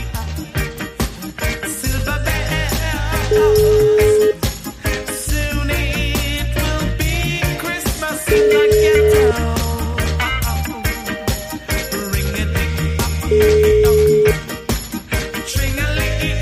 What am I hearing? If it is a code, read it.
Ringback tone (Europe, 425 Hz), with noise.